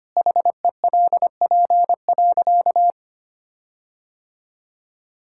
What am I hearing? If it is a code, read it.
Morse code, heard as a keyed tone, about 25 wpm, 700 Hz. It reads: HELP.
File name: practice / 25wpm / Sentences with 4-Word Phrases - 25wpm